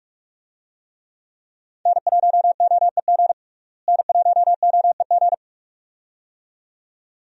Morse code, heard as a keyed tone, about 47 wpm, 700 Hz. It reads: N1OEG N1OEG
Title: USA Call Signs - 5 Characters - Repeated 2x - 45wpm